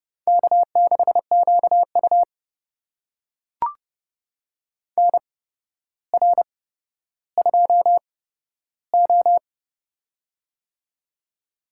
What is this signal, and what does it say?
Morse code, heard as a keyed tone, about 29 wpm, spaced as for 13 wpm, 700 Hz. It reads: K6QU NR2O
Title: USA Call Signs - 4 Characters - 8x Character Spacing - 30wpm